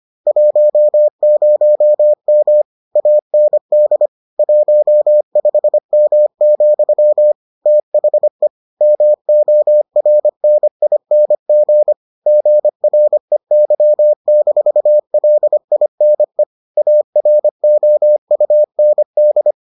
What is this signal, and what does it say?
Morse code, heard as a keyed tone, about 25 wpm, 600 Hz. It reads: 10M AND 15M, THE MORNING GREY-LINE AROUND